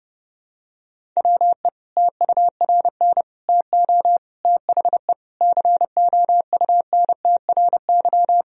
Morse code, heard as a keyed tone, about 30 wpm, 700 Hz. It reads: WE TURN TO THE COUNTRY
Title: Sentences from Top 1000 Words - Review Part 1 - 30wpm